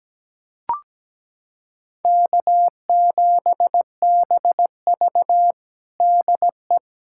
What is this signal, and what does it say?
Morse code, heard as a keyed tone, about 17 wpm, 700 Hz. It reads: K7BV DE
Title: Practice QSO - Exchange 4 - Ending - 17wpm